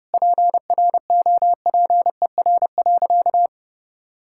Morse code, heard as a keyed tone, about 30 wpm, 700 Hz. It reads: PROPER.
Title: Sentences from Top 1000 Words - Review Part 5 - 30wpm